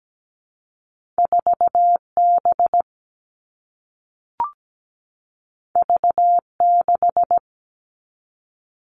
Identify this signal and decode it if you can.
Morse code, heard as a keyed tone, about 17 wpm, 700 Hz. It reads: 4B V6
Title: All Cut Numbers - 2-Digit 17wpm